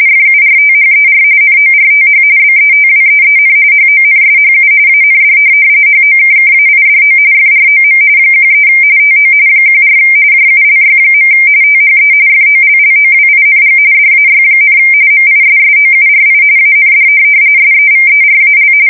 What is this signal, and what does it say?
RTTY, about 45.45 baud. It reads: FURTHER NORTH BETWEEN HONSHU AND HOKKAIDO FROM 13 TO 14 JULY. THE FOLLOWING DAY, MISSOURI AND SEVERAL
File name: fsk170 / USS-Missouri